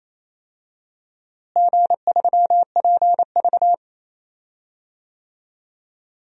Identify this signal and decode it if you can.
Morse code, heard as a keyed tone, about 28 wpm, 700 Hz. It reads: G3PV
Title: DXCC-100 Call Signs - 4 Characters - Repeated 3x - 28wpm